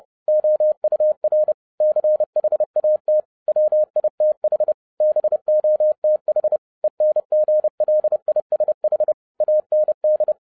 Morse code, heard as a keyed tone, about 30 wpm, 610 Hz. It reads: OUR CHAT WITH BOTH ENGLISH AND